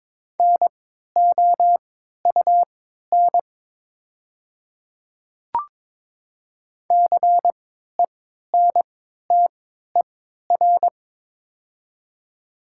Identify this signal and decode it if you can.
Morse code, heard as a keyed tone, about 22 wpm, spaced as for 12 wpm, 700 Hz. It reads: NOUN CENTER
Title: 3x Character Spacing - Top 500 Words - 22wpm